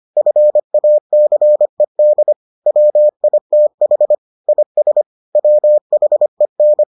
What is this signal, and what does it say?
Morse code, heard as a keyed tone, about 25 wpm, 600 Hz. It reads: FACED WITH IS WHEN